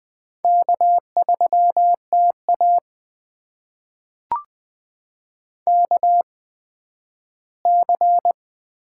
Morse code, heard as a keyed tone, about 20 wpm, spaced as for 12 wpm, 700 Hz. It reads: K3TA KC
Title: USA Call Signs - 8x Character Spacing - 20wpm